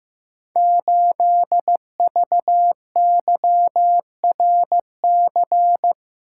Morse code, heard as a keyed tone, about 15 wpm, 700 Hz. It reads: 8VYRC
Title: Code Groups - 5 Letter:Numbers - 15wpm